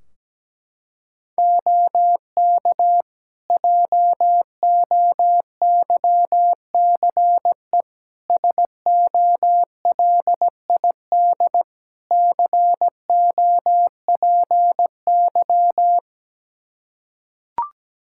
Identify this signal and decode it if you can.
Morse code, heard as a keyed tone, about 17 wpm, 700 Hz. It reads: OK JOYCE SOLID COPY
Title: Practice QSO - Exchange 3 - More Info - 17wpm